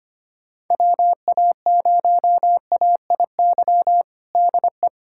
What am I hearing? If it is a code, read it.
Morse code, heard as a keyed tone, about 25 wpm, 700 Hz. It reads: WA0AIY DE
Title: USA Callsign DE Callsign 25wpm